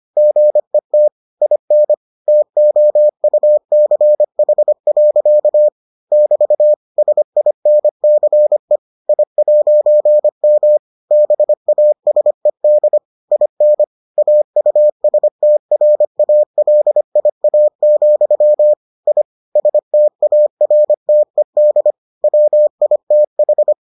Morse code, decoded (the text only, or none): GET IN TOUCH. <BT> SINCE I'M BASED IN AUSTRALIA, I STARTED WITH